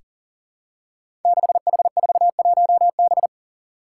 Morse code, heard as a keyed tone, about 40 wpm, 700 Hz. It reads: 6H41B
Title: Five Challenging Letter-Number Combinations 40wpm